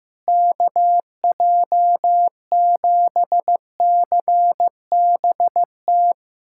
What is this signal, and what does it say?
Morse code, heard as a keyed tone, about 15 wpm, 700 Hz. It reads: KJ7CBT
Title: SOTA QSO Practice 15wpm